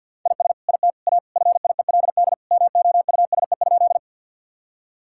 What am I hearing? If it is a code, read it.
Morse code, heard as a keyed tone, about 47 wpm, 700 Hz. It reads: IS IT A FIELD MOUSE?
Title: Sentences from Top 1000 Words - Review Part 5 - 50wpm